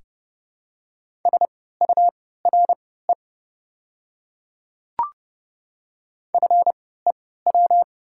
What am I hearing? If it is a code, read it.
Morse code, heard as a keyed tone, about 30 wpm, spaced as for 17 wpm, 700 Hz. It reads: SURE FEW
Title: 3x Character Spacing - Top 300 Words - 30wpm